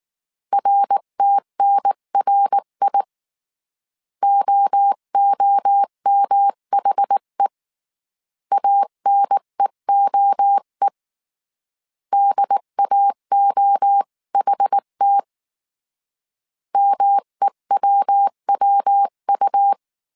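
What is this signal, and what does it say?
Morse code, heard as a keyed tone, about 19 wpm, 800 Hz. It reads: RTNRI OOMHE ANEOE DAOHT MEWWU
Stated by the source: Morse code dictation at 18 wpm from the IW3SGT QRS Morse code training course, five-character groups of 300 characters.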